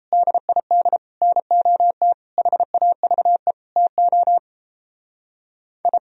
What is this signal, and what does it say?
Morse code, heard as a keyed tone, about 35 wpm, 700 Hz. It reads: DID NOT HAVE TO I
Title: Speed-Racing - Sets of 5 Words 28wpm